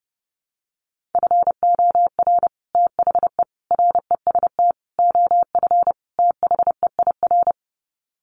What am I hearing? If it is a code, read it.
Morse code, heard as a keyed tone, about 30 wpm, 700 Hz. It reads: FOR THE REST OF THEIR
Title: Sets of 5 Words Encore2 30wpm